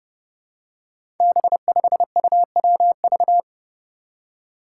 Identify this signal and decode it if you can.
Morse code, heard as a keyed tone, about 29 wpm, 700 Hz. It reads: B5UWV
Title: Five Challenging Letter-Number Combinations 30wpm